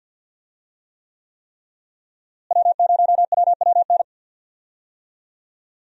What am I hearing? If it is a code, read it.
Morse code, heard as a keyed tone, about 50 wpm, 700 Hz. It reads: W0WWN
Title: USA Call Signs - 5 Characters - Repeated 3x - 50wpm